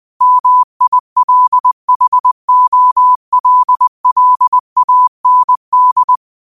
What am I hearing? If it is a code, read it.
Morse code, heard as a keyed tone, about 20 wpm, 1000 Hz. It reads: MILHOLLAND
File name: qotdfiles / 20wpm